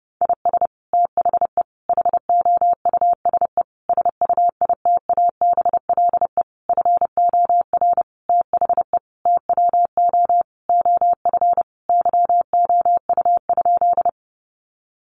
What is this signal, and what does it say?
Morse code, heard as a keyed tone, about 30 wpm, 700 Hz. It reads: IS THE HOUSE SUITABLE FOR THE TWO OF YOU?
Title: Sentences from Top 1400 Words 30wpm